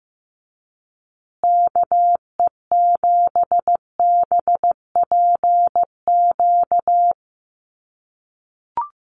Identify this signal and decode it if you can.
Morse code, heard as a keyed tone, about 15 wpm, 700 Hz. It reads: KE7BPQ
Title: USA Call Signs Repeated 2x 15wpm